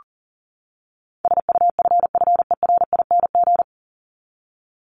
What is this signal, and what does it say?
Morse code, heard as a keyed tone, about 40 wpm, 700 Hz. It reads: SUFFERING
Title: ING Suffix - Top 100 Words with Base Words 40wpm